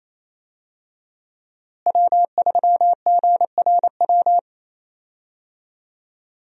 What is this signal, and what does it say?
Morse code, heard as a keyed tone, about 27 wpm, 700 Hz. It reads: W3GRW
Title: USA Call Signs - 5 Characters - Repeated 2x - 28wpm